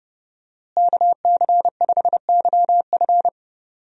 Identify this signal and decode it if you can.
Morse code, heard as a keyed tone, about 30 wpm, 700 Hz. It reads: KC5YF
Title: USA Call Signs - 5 Characters - Repeated 2x - 30wpm